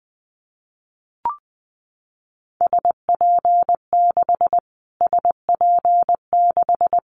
Morse code, heard as a keyed tone, about 19 wpm, 700 Hz. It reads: SP6 SP6
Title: Callsign Prefixes for Top 25 DXCC - Repeated 3x - 20wpm